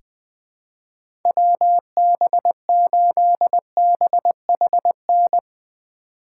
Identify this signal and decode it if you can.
Morse code, heard as a keyed tone, about 21 wpm, 700 Hz. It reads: WB8BHN
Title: 2023 SOTA Chaser Call Signs - 20wpm